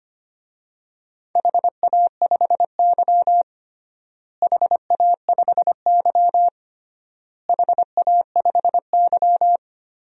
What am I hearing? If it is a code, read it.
Morse code, heard as a keyed tone, about 25 wpm, 700 Hz. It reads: HA5Y HA5Y HA5Y
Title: DXCC-100 Call Signs - 4 Characters - Repeated 3x - 25wpm